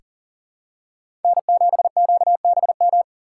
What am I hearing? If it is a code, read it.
Morse code, heard as a keyed tone, about 40 wpm, 700 Hz. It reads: N7QBM
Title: USA Call Signs - 5 Characters - 2x Character Spacing - 40wpm